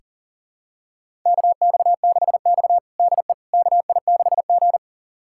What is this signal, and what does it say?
Morse code, heard as a keyed tone, about 40 wpm, 700 Hz. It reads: KX6X DE KI6G
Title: USA Callsign DE Callsign - 4 Characters - 40wpm